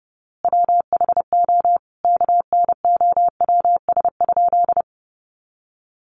Morse code, heard as a keyed tone, about 30 wpm, 700 Hz. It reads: WHO KNOWS?
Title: A Sound Of Thunder 30wpm